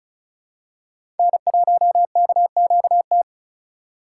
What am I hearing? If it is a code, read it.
Morse code, heard as a keyed tone, about 36 wpm, 700 Hz. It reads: N1KQT
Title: USA Call Signs - 4x Character Spacing - 35wpm